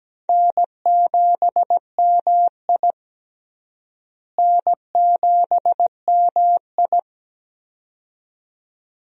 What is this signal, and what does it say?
Morse code, heard as a keyed tone, about 17 wpm, 700 Hz. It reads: N7MI N7MI
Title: USA Call Signs - 4 Characters - Repeated 2x - 17wpm